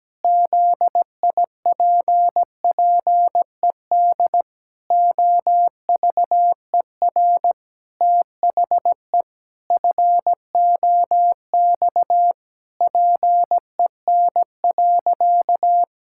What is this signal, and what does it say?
Morse code, heard as a keyed tone, about 17 wpm, 700 Hz. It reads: ZIPPED OVER THE FOX PEN.
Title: Pangram Challenge v2 - 17wpm